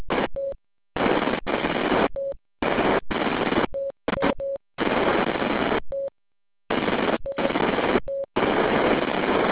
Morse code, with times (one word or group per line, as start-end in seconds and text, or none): none